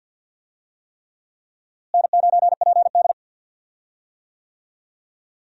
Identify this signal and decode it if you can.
Morse code, heard as a keyed tone, about 49 wpm, 700 Hz. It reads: N9PD
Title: USA Call Signs - 4 Characters - Repeated 3x - 50wpm